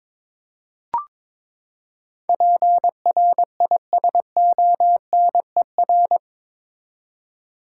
Morse code, heard as a keyed tone, about 21 wpm, 700 Hz. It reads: PRISONER
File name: practice / 22wpm / Eight Letter Words - 22wpm